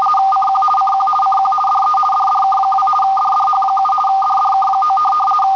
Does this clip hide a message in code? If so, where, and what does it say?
Yes: RTTY from 0.3 to 5.0 s, about 50 baud. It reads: CQ CQ CQ DE DDK2 DDH7 DDK9